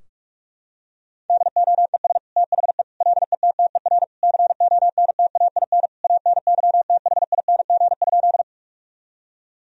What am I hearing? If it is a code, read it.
Morse code, heard as a keyed tone, about 47 wpm, 700 Hz. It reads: DOES THE LETTER CONTAIN ANYTHING?
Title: Sentences from Top 1000 Words - Review Part 2 - 45wpm